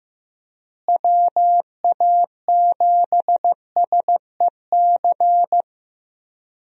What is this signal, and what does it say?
Morse code, heard as a keyed tone, about 15 wpm, 700 Hz. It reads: WA7SEC
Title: USA Call Signs - 6 Characters - Repeated 3x - 15wpm